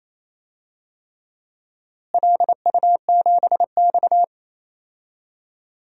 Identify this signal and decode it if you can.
Morse code, heard as a keyed tone, about 28 wpm, 700 Hz. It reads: LU7X
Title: DXCC-100 Call Signs - 4 Characters - Repeated 3x - 28wpm